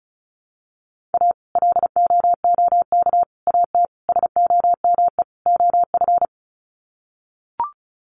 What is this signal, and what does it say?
Morse code, heard as a keyed tone, about 34 wpm, 700 Hz. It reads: A LOOK AT SOME OF
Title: Sets of 5 Words Encore2 35wpm